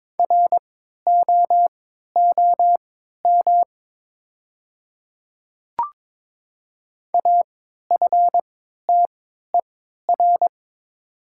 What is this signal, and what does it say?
Morse code, heard as a keyed tone, about 22 wpm, spaced as for 12 wpm, 700 Hz. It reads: ROOM AFTER
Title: 3x Character Spacing - Top 500 Words - 22wpm